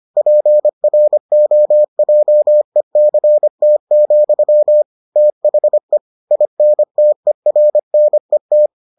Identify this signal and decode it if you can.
Morse code, heard as a keyed tone, about 25 wpm, 600 Hz. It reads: PROJECT, THE INTERNET